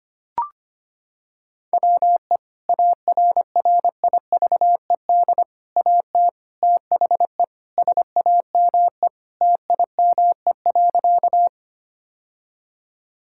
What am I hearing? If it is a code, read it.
Morse code, heard as a keyed tone, about 25 wpm, 700 Hz. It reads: WE ARRIVED AT THE SAME TIME.
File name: practice / 25wpm / Sentences with 3-Word Phrases - 25wpm